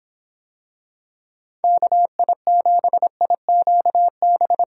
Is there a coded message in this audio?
Yes: Morse code from 1.6 to 4.6 s, heard as a keyed tone, about 26 wpm, 700 Hz. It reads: KI7IQB